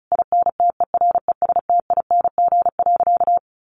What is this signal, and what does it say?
Morse code, heard as a keyed tone, about 34 wpm, 700 Hz. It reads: INTERESTING.
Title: Sentences from Top 1000 Words 35wpm